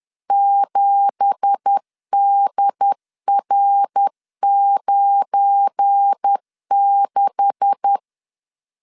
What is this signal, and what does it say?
Morse code, heard as a keyed tone, about 10 wpm, 800 Hz. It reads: 7DR96
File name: LEZIONI / 9 2867 10wpm 2733